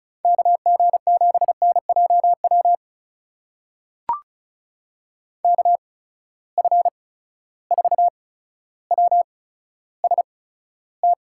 Morse code, heard as a keyed tone, about 36 wpm, spaced as for 15 wpm, 700 Hz. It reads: KG7NJW KF4WST
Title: USA Call Signs - 6 Characters - 8x Character Spacing - 35wpm